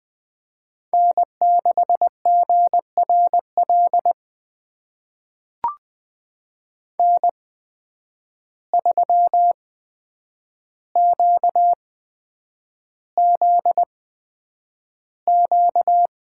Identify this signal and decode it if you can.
Morse code, heard as a keyed tone, about 20 wpm, spaced as for 9 wpm, 700 Hz. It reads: N6GRL N3QZQ
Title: USA Call Signs - 5 Characters - 8x Character Spacing - 20wpm